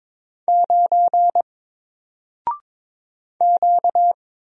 Morse code, heard as a keyed tone, about 22 wpm, 700 Hz. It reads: 9 Q
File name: Single-Letter-Number-22wpm